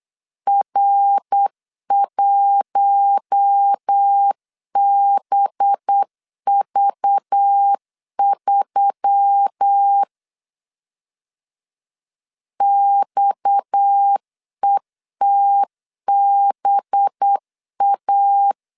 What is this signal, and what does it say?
Morse code, heard as a keyed tone, about 8 wpm, 800 Hz. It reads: R1BV3 XETBA
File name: 8wpm